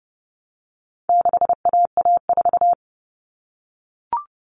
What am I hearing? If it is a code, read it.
Morse code, heard as a keyed tone, about 30 wpm, 700 Hz. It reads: 6AA4
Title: All Cut Numbers - 4-Digit 30wpm